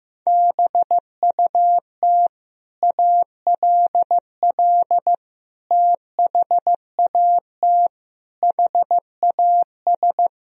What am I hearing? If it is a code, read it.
Morse code, heard as a keyed tone, about 16 wpm, 700 Hz. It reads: BUT ALL THAT HAS
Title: Sentences from Top 1000 Words - Review Part 3 - 15wpm